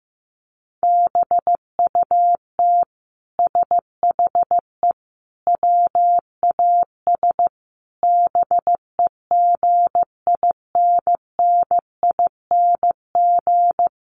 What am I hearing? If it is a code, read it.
Morse code, heard as a keyed tone, about 15 wpm, 700 Hz. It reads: BUT SHE WAS BEGINNING
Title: Sentences from Top 1500 Words 15wpm